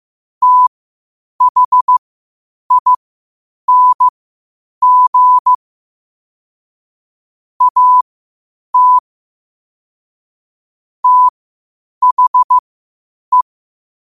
Morse code, heard as a keyed tone, about 15 wpm, spaced as for 8 wpm, 1000 Hz. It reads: THING AT THE
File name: qotdfiles / 8wpm